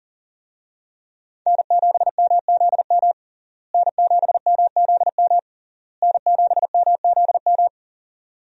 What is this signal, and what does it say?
Morse code, heard as a keyed tone, about 40 wpm, 700 Hz. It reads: N7MZM N7MZM N7MZM
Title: USA Call Signs - 5 Characters - Repeated 3x - 40wpm